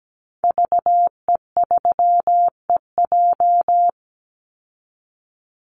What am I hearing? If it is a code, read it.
Morse code, heard as a keyed tone, about 17 wpm, 700 Hz. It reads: VE3EJ
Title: Call Signs Repeated 3x - 17wpm